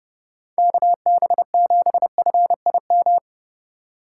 Morse code, heard as a keyed tone, about 30 wpm, 700 Hz. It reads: KB7FIM